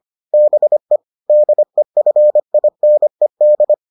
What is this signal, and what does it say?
Morse code, heard as a keyed tone, about 25 wpm, 600 Hz. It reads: BE DEFINED